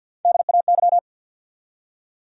Morse code, heard as a keyed tone, about 69 wpm, spaced as for 59 wpm, 700 Hz. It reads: DAY
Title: Speed-Racing Top 100 Words 50wpm